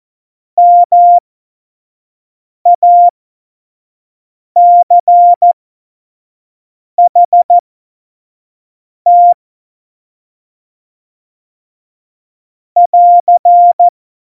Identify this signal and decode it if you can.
Morse code, heard as a keyed tone, about 14 wpm, spaced as for 5 wpm, 700 Hz. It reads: MACHT <AR>
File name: Morsekurs / Witze / Witz 11 Juli 17, Tempo 25 F70